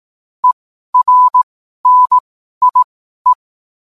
Morse code, heard as a keyed tone, about 18 wpm, spaced as for 12 wpm, 1000 Hz. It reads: ERNIE